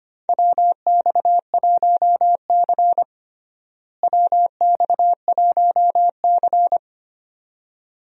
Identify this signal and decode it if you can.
Morse code, heard as a keyed tone, about 25 wpm, 700 Hz. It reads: WX1C WX1C
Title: USA Call Signs - 4 Characters - Repeated 2x - 25wpm